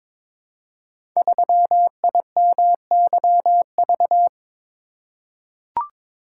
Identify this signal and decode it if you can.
Morse code, heard as a keyed tone, about 22 wpm, 700 Hz. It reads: 3IMYV
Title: Code Groups - 5 Letter:Numbers - 22wpm